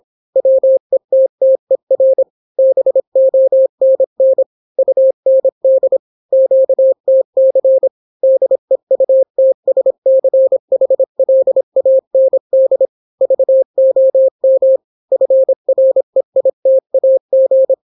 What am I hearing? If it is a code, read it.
Morse code, heard as a keyed tone, about 26 wpm, 530 Hz. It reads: WETTER BONN UND QTC DEUTSCHLAND VOM FREITAG